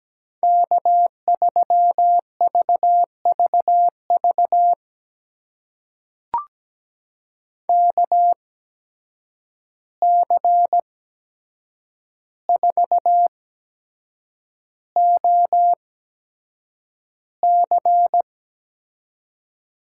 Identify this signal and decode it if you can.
Morse code, heard as a keyed tone, about 17 wpm, spaced as for 7 wpm, 700 Hz. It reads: K3VVV KC4OC
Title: USA Call Signs - 8x Character Spacing - 17wpm